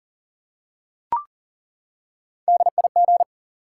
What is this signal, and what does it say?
Morse code, heard as a keyed tone, about 38 wpm, 700 Hz. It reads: DIG
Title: Three Letter Words - 40wpm